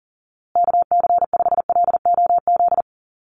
Morse code, heard as a keyed tone, about 40 wpm, 700 Hz. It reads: KC5LOZ
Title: USA Call Signs Repeated 2x 40wpm